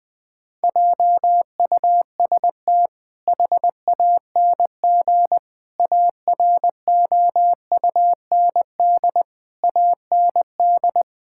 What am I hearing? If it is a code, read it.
Morse code, heard as a keyed tone, about 20 wpm, 700 Hz. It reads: JUST HANG AROUND AND
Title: Sentences from Top 1600 Words 20wpm